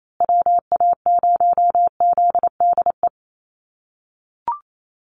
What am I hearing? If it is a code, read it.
Morse code, heard as a keyed tone, about 28 wpm, 700 Hz. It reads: WA0ZDE
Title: USA Call Signs Repeated 3x 28wpm